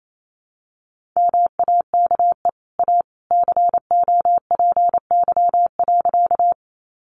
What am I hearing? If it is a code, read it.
Morse code, heard as a keyed tone, about 28 wpm, 700 Hz. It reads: MAKE A COPY.
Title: Sentences from Top 1400 Words 28wpm